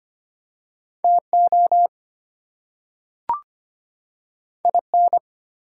Morse code, heard as a keyed tone, about 25 wpm, 700 Hz. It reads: TO IN
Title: Two Letter Words - 25wpm